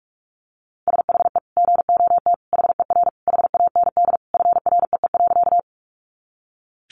Morse code, heard as a keyed tone, about 44 wpm, 700 Hz. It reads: SHE GOT HER HAND FREE.
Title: Sentences from Top 500 Words 45wpm